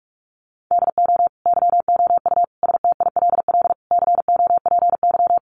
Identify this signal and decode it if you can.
Morse code, heard as a keyed tone, about 44 wpm, 700 Hz. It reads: DO YOU STILL COPY
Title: Sentences from Top 1400 Words 45wpm